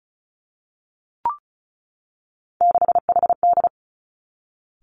Morse code, heard as a keyed tone, about 36 wpm, 700 Hz. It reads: 6HD
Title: Three Challenging Letter-Number Combinations 35wpm